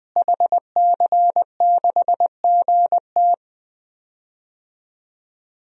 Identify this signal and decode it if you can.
Morse code, heard as a keyed tone, about 20 wpm, 700 Hz. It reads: HC6GT
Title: DXCC-100 Call Signs - 5 Characters - Repeated 3x - 20wpm